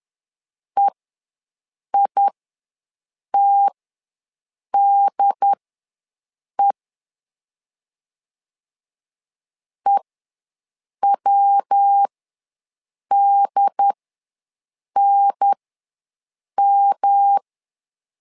Morse code, heard as a keyed tone, about 10 wpm, spaced as for 6 wpm, 800 Hz. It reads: EITDE EWDNM
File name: LEZIONI / 3 MWRO / 3 MW ws 10wpm 2213